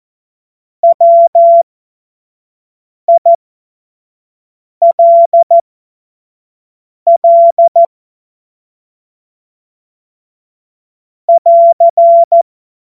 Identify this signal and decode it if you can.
Morse code, heard as a keyed tone, about 14 wpm, spaced as for 5 wpm, 680 Hz. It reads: WILL <AR>
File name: 25 F70